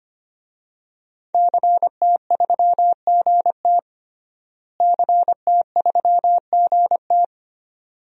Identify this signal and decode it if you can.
Morse code, heard as a keyed tone, about 25 wpm, 700 Hz. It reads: CT3GT CT3GT
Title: DXCC-100 Call Signs - 5 Characters - Repeated 2x - 25wpm